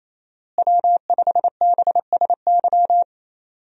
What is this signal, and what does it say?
Morse code, heard as a keyed tone, about 28 wpm, 700 Hz. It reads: W5BSY
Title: USA Call Signs - 5 Characters - Repeated 2x - 28wpm